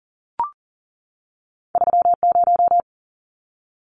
Morse code, heard as a keyed tone, about 40 wpm, 700 Hz. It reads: 30
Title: Numbers - 2-Digit 40wpm